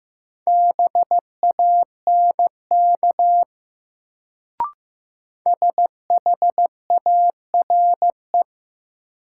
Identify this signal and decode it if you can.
Morse code, heard as a keyed tone, about 15 wpm, 700 Hz. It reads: BANK SHARE
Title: Sentences from Top 900 Words - New Word Drill - 15wpm